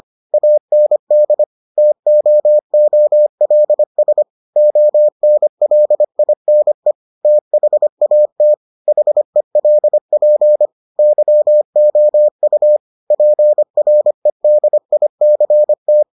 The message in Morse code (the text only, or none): AND TOOLS ONLINE THAT HELP YOU PREDICT